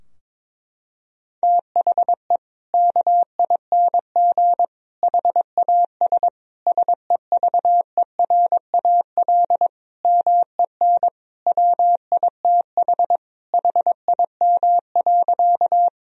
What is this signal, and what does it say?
Morse code, heard as a keyed tone, about 22 wpm, 700 Hz. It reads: THE KING HAS SEVERAL MEN WITH HIM.
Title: Sentences from Top 1000 Words - Review Part 2 - 22wpm